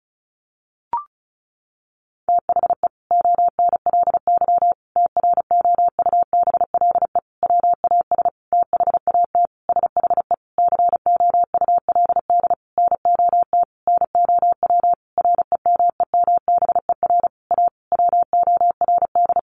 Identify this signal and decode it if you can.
Morse code, heard as a keyed tone, about 34 wpm, 700 Hz. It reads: THE ONLY TROUBLE WAS THAT SHE COULD NOT NOW REMEMBER A WORD
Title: Sentences from Top 1200 Words 35wpm